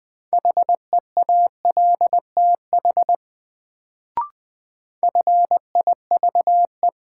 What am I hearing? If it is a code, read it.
Morse code, heard as a keyed tone, about 21 wpm, 700 Hz. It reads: HEALTH FIVE
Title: Sets of 5 Words - New Word Drill - 20wpm